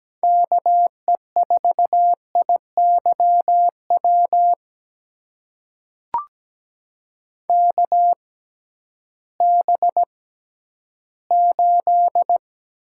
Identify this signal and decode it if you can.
Morse code, heard as a keyed tone, about 17 wpm, spaced as for 11 wpm, 700 Hz. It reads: KE4IYW KB8
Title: USA Call Signs - 6 Characters - 6x Character Spacing - 17wpm